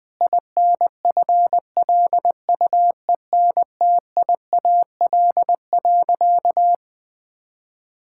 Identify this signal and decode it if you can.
Morse code, heard as a keyed tone, about 21 wpm, 700 Hz. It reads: INFLUENTIAL.
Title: Sentences with 4-Word Phrases - Spoken Beginning - 20wpm